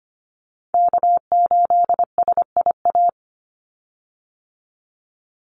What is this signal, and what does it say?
Morse code, heard as a keyed tone, about 25 wpm, 700 Hz. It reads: K8SIA
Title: Call Signs Repeated 2x - 25wpm